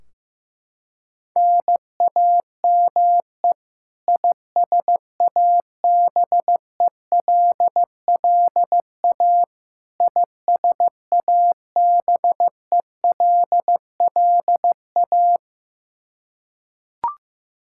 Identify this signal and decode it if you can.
Morse code, heard as a keyed tone, about 15 wpm, 700 Hz. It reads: NAME ISABELLA ISABELLA
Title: K1USN SST QSO Practice 15wpm